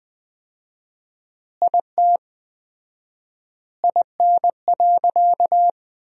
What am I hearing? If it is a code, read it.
Morse code, heard as a keyed tone, about 20 wpm, 700 Hz. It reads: IT IN.
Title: 4x Word-Spacing - Sentences from Top 200 Words - 20wpm